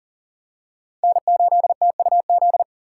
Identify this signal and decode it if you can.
Morse code, heard as a keyed tone, about 40 wpm, 700 Hz. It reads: N8TUZ
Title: USA Call Signs - 5 Characters - 4x Character Spacing - 40wpm